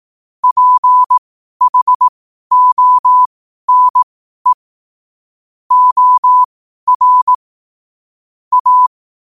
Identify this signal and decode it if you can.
Morse code, heard as a keyed tone, about 18 wpm, spaced as for 12 wpm, 1000 Hz. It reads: PHONE OR A
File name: qotdfiles / 12wpm